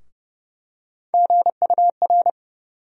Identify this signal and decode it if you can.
Morse code, heard as a keyed tone, about 32 wpm, 700 Hz. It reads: GUR
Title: USA Call Sign Postfixes - 3 Letters 30wpm